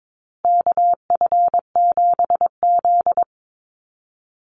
Morse code, heard as a keyed tone, about 22 wpm, 700 Hz. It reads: KF7Z
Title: Call Signs Repeated 3x - 22wpm